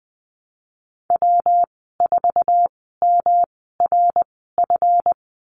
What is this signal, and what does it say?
Morse code, heard as a keyed tone, about 19 wpm, spaced as for 14 wpm, 700 Hz. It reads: W4MRF
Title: 154 Lesson 25g - Letter p - Call Signs 20wpm